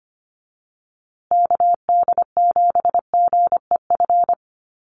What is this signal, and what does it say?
Morse code, heard as a keyed tone, about 25 wpm, 700 Hz. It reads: KD7GEF
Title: USA Call Signs Repeated 3x 25wpm